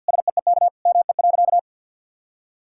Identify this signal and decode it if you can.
Morse code, heard as a keyed tone, about 50 wpm, 700 Hz. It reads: SEEK ME.
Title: Sentences from Top 1600 Words 50wpm